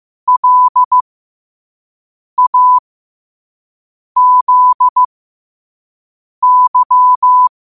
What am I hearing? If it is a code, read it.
Morse code, heard as a keyed tone, about 15 wpm, spaced as for 5 wpm, 1000 Hz. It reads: LAZY
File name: 5wpm